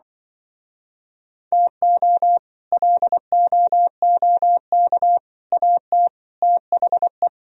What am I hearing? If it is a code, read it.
Morse code, heard as a keyed tone, about 24 wpm, 700 Hz. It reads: TO LOOK AT THE
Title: Speed-Racing - Sets of 4 Words 20wpm